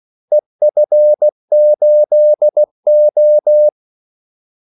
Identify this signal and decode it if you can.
Morse code, heard as a keyed tone, about 16 wpm, 600 Hz. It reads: EF8O
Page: Kaldesignaler 16 wpm